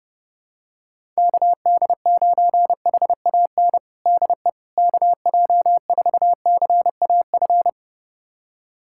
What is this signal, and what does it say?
Morse code, heard as a keyed tone, about 28 wpm, 700 Hz. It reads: KD9HAN DE KJ4CAF
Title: USA Callsign DE Callsign 30wpm